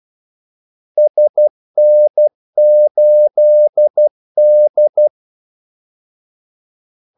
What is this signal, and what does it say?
Morse code, heard as a keyed tone, about 12 wpm, 600 Hz. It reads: SN8D
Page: 2 ekstra ordmellemrum mellem kaldesignalerne